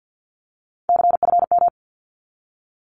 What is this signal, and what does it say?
Morse code, heard as a keyed tone, about 48 wpm, 700 Hz. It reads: CFM
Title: Abbreviations 50wpm